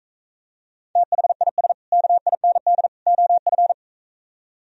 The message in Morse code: THIS KIND OF

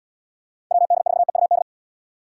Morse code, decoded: WD4EUD

